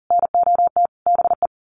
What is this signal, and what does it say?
Morse code, heard as a keyed tone, about 40 wpm, 700 Hz. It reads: NOT BE
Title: Sets of 5 Words Encore2 40wpm